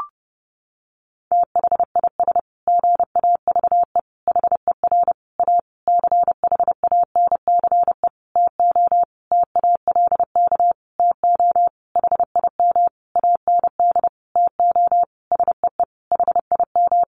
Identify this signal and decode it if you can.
Morse code, heard as a keyed tone, about 30 wpm, 700 Hz. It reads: THIS GAVE HER A CHANCE TO TALK TO HIM AND TO SEE HIM
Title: Sentences from Top 1400 Words 30wpm